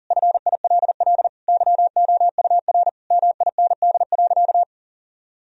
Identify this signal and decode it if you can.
Morse code, heard as a keyed tone, about 40 wpm, 700 Hz. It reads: FILL YOUR MIND.
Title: Sentences from Top 1000 Words - Review Part 3 - 40wpm